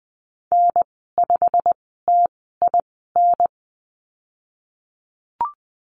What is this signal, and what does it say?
Morse code, heard as a keyed tone, about 20 wpm, spaced as for 15 wpm, 700 Hz. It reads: N5TIN
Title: 128 Lesson 21g - Number 6 - Call Signs 20wpm